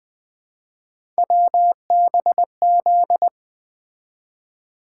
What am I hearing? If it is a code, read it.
Morse code, heard as a keyed tone, about 20 wpm, 700 Hz. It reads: WBZ
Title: Three Letter-Number Combinations - 20wpm